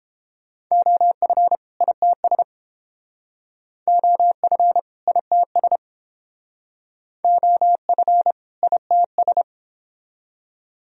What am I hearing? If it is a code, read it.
Morse code, heard as a keyed tone, about 30 wpm, 700 Hz. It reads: OF ITS OF ITS OF ITS